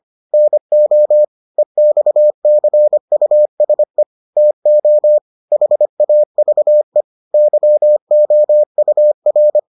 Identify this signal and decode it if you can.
Morse code, heard as a keyed tone, about 25 wpm, 600 Hz. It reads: NO EXCUSE TO HAVE YOUR